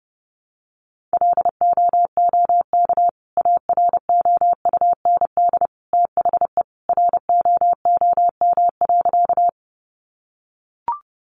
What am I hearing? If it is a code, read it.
Morse code, heard as a keyed tone, about 30 wpm, 700 Hz. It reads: LOOK AROUND THE ROOM.
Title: Sentences from Top 600 Words 30wpm